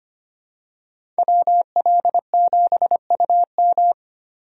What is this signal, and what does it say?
Morse code, heard as a keyed tone, about 25 wpm, 700 Hz. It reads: WL7UM
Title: USA Call Signs - 5 Characters - Repeated 3x - 25wpm